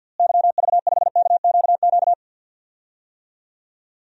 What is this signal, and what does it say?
Morse code, heard as a keyed tone, about 52 wpm, 700 Hz. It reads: YV5KQQ
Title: DXCC-100 Call Signs - 6 Characters - Repeated 3x - 50wpm